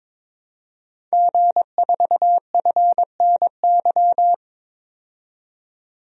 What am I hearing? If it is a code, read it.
Morse code, heard as a keyed tone, about 22 wpm, 700 Hz. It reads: G4FNY
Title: DXCC-100 Call Signs - 5 Characters - 22wpm